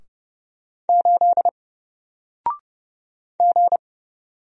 Morse code, heard as a keyed tone, about 30 wpm, 700 Hz. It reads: 8 G
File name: Single-Letter-Number-30wpm